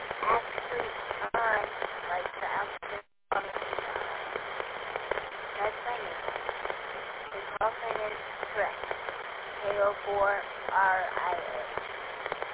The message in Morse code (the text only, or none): none